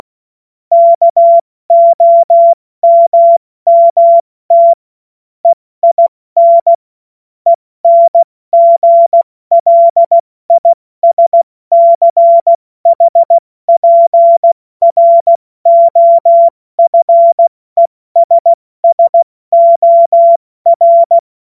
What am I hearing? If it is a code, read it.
Morse code, heard as a keyed tone, about 16 wpm, 680 Hz. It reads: KOMMT EIN ENGLISCHPROFESSOR
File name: Morsekurs / Witze / Witz 16. Dezember 2025 70 F80